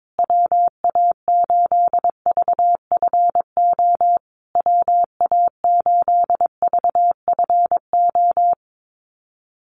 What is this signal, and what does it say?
Morse code, heard as a keyed tone, about 22 wpm, 700 Hz. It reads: WA8VFO WA8VFO
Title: USA Call Signs Repeated 3x 22wpm